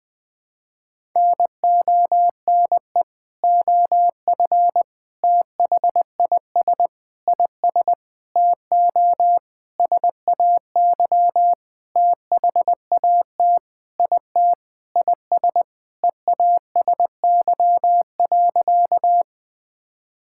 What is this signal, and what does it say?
Morse code, heard as a keyed tone, about 20 wpm, 700 Hz. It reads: NONE OF THIS IS TO SAY THAT IT IS EASY.
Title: Sentences with 7-Word Phrases - 20wpm